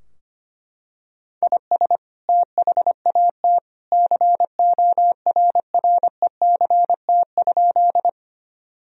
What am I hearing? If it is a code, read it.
Morse code, heard as a keyed tone, about 25 wpm, 700 Hz. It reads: IS THAT CORRECT?
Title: Sentences from Top 1000 Words - Review Part 2 - 25wpm